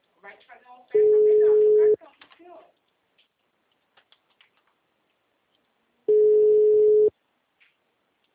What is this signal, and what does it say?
Ringback tone (Europe, 425 Hz), with speech.
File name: out- -2013- - - .4627